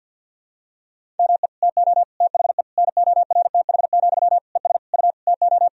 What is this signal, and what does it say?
Morse code, heard as a keyed tone, about 46 wpm, 700 Hz. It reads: ME TO THE NORTH, ES U TO
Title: Sentences from Top 400 Words and Abbreviations 50wpm